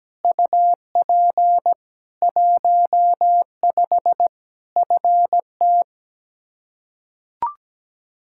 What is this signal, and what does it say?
Morse code, heard as a keyed tone, about 17 wpm, 700 Hz. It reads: UP 15 FT
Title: Practice QSO - Exchange 2 - Info - 17wpm